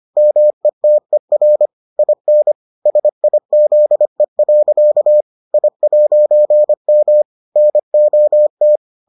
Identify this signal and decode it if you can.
Morse code, heard as a keyed tone, about 25 wpm, 600 Hz. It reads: METER IN SIZE. I'M NOT